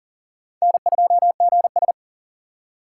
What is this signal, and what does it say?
Morse code, heard as a keyed tone, about 40 wpm, 700 Hz. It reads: N2GS
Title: USA Call Signs - 4 Characters - 3x Character Spacing - 40wpm